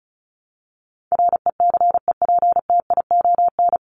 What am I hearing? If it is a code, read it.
Morse code, heard as a keyed tone, about 34 wpm, 700 Hz. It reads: RECEPTION
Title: Top 1751-2000 Words 35wpm